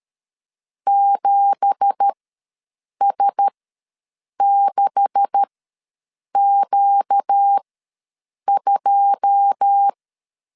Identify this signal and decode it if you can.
Morse code, heard as a keyed tone, about 13 wpm, spaced as for 7 wpm, 800 Hz. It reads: 7S6Q2
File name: LEZIONI / 9 2867 ws 12wpm 3431